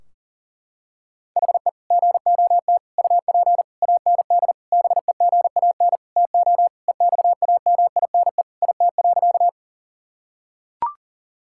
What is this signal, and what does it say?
Morse code, heard as a keyed tone, about 40 wpm, 700 Hz. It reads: HE GOT UP AND BEGAN TO EXAMINE IT.
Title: Sentences from Top 1600 Words 40wpm